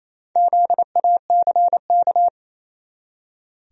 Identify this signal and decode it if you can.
Morse code, heard as a keyed tone, about 28 wpm, 700 Hz. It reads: ZACK
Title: USA Common Names - Max 5 Letters - 28wpm